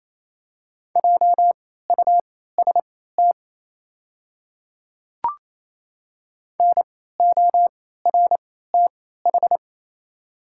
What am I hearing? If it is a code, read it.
Morse code, heard as a keyed tone, about 28 wpm, spaced as for 16 wpm, 700 Hz. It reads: JUST NORTH
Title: 3x Character Spacing - Top 500 Words - 28wpm